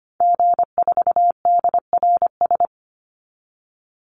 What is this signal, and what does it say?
Morse code, heard as a keyed tone, about 25 wpm, 700 Hz. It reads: G4DRS
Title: Call Signs - 25wpm